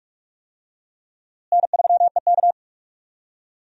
Morse code, heard as a keyed tone, about 46 wpm, 700 Hz. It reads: N3EK
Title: USA Call Signs - 4 Characters - Repeated 2x - 45wpm